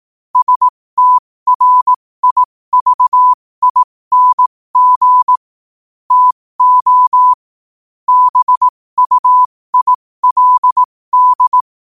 Morse code, heard as a keyed tone, about 18 wpm, spaced as for 15 wpm, 1000 Hz. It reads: STRIVING TO BUILD
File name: qotdfiles / 15wpm